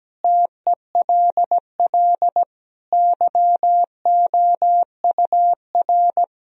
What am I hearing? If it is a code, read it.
Morse code, heard as a keyed tone, about 17 wpm, 700 Hz. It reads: TELL YOUR
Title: Sentences from Top 1000 Words - Review Part 4 - 17wpm